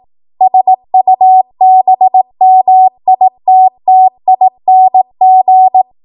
Morse code, heard as a keyed tone, about 18 wpm, 750 Hz. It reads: SUBMITTING
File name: Morse / 18WPM